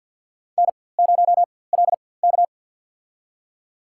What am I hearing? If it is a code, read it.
Morse code, heard as a keyed tone, about 53 wpm, spaced as for 24 wpm, 700 Hz. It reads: N0LK
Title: USA Call Signs - 4x Character Spacing - 50wpm